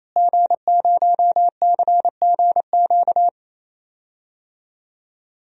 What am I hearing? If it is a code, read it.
Morse code, heard as a keyed tone, about 28 wpm, 700 Hz. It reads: G0CGQ
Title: DXCC-100 Call Signs - 5 Characters - Repeated 3x - 28wpm